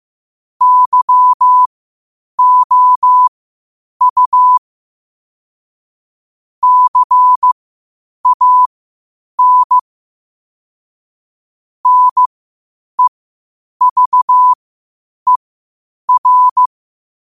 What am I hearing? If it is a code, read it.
Morse code, heard as a keyed tone, about 15 wpm, spaced as for 8 wpm, 1000 Hz. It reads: YOU CAN NEVER
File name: qotdfiles / 8wpm